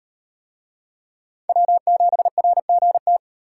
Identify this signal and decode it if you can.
Morse code, heard as a keyed tone, about 38 wpm, 700 Hz. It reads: W7RGT